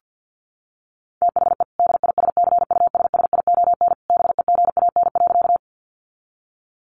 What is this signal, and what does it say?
Morse code, heard as a keyed tone, about 49 wpm, 700 Hz. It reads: THE DISCUSSION BEGAN.
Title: Sentences from Top 1100 Words 50wpm